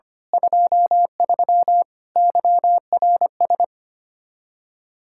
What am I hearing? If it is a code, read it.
Morse code, heard as a keyed tone, about 25 wpm, 700 Hz. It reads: 23 YRS
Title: Speed-Racing Common QSO Phrases - 25wpm